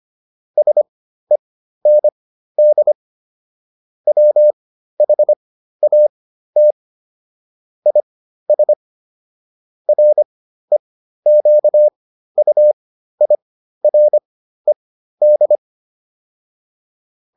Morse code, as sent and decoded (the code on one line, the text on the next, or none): ... -. -.. / .-- .... .- - / .. ... / .-. --.- ..- .. .-. -..
SEND WHAT IS REQUIRED